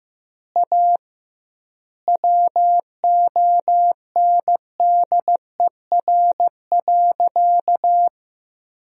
Morse code, heard as a keyed tone, about 15 wpm, 700 Hz. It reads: A WONDER.
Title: 2x Word-Spacing - Sentences from Top 500 Words - 15wpm